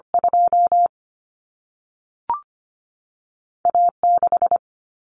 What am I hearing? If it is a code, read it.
Morse code, heard as a keyed tone, about 25 wpm, 700 Hz. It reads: 2 A 6
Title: Common Cut Numbers - 2-Digit 25wpm